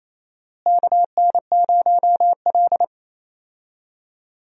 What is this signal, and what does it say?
Morse code, heard as a keyed tone, about 28 wpm, 700 Hz. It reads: KN0L
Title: USA Call Signs - 4 Characters - Repeated 2x - 28wpm